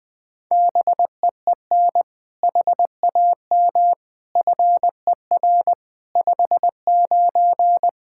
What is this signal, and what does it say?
Morse code, heard as a keyed tone, about 20 wpm, 700 Hz. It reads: BEEN HAM FER 59
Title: Practice QSO - Exchange 3 - More Info - 20wpm